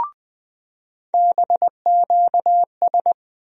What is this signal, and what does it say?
Morse code, heard as a keyed tone, about 20 wpm, 700 Hz. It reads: BQS
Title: USA Call Sign Postfixes - 3 Letters 20wpm